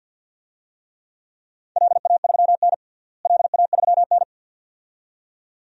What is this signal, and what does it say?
Morse code, heard as a keyed tone, about 52 wpm, 700 Hz. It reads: LA3N LA3N
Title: DXCC-100 Call Signs - 4 Characters - Repeated 2x - 50wpm